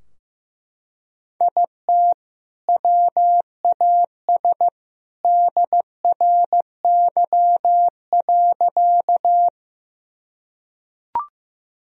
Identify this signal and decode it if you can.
Morse code, heard as a keyed tone, about 14 wpm, 700 Hz. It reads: IT WAS DRY.
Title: Sentences from Top 1000 Words - Review Part 3 - 15wpm